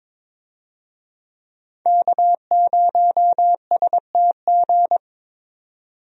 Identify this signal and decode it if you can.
Morse code, heard as a keyed tone, about 22 wpm, 700 Hz. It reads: K0STG
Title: USA Call Signs - 5 Characters - 22wpm